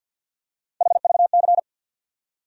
Morse code, heard as a keyed tone, about 47 wpm, 700 Hz. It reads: HVC
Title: USA Call Sign Postfixes - 3 Letters 50wpm